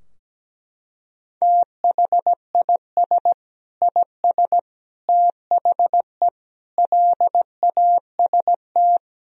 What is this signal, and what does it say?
Morse code, heard as a keyed tone, about 17 wpm, 700 Hz. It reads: THIS IS THE LAST
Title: Sentences from Top 1000 Words - Review Part 4 - 17wpm